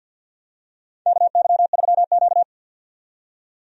Morse code, heard as a keyed tone, about 49 wpm, 700 Hz. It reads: KY3Q
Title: USA Call Signs - 4 Characters - 50wpm